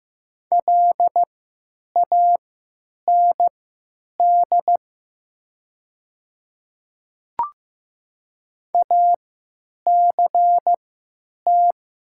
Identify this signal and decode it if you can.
Morse code, heard as a keyed tone, about 15 wpm, spaced as for 9 wpm, 700 Hz. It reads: LAND ACT
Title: 3x Character Spacing - Top 300 Words - 15wpm